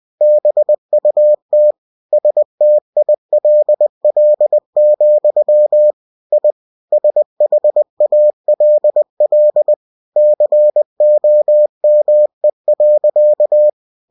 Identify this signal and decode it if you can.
Morse code, heard as a keyed tone, about 20 wpm, 600 Hz. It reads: BUT STILL, I SHALL COME.
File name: cw-books / winnie-the-pooh / winnie-20wpm-page-0105